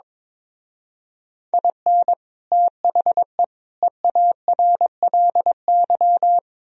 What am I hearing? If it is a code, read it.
Morse code, heard as a keyed tone, about 24 wpm, 700 Hz. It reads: IN THE EARLY
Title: Speed-Racing - Sets of 3 Words 22wpm